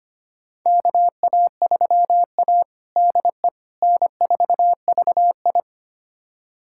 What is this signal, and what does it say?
Morse code, heard as a keyed tone, about 25 wpm, 700 Hz. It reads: KA3A DE N4VI
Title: USA Callsign DE Callsign - 4 Characters - 25wpm